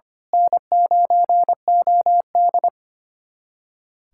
Morse code, heard as a keyed tone, about 25 wpm, 700 Hz. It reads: N9OD
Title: USA Callsign DE Callsign - 4 Characters - 25wpm